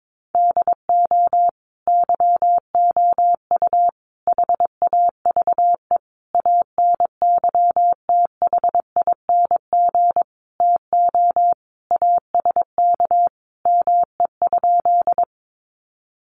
Morse code, heard as a keyed tone, about 22 wpm, 700 Hz. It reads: DO YOU HAVE ANYTHING TO ASK ME?
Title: Sentences from Top 1000 Words 22wpm